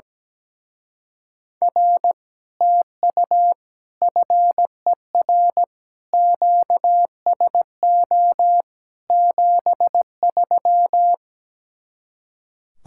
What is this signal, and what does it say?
Morse code, heard as a keyed tone, about 17 wpm, 700 Hz. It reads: R TU FER QSO 73
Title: Speed-Racing Common QSO Phrases - 17wpm